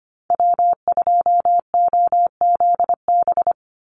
Morse code, heard as a keyed tone, about 25 wpm, 700 Hz. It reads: W2OZB
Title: Call Signs Repeated 2x - 25wpm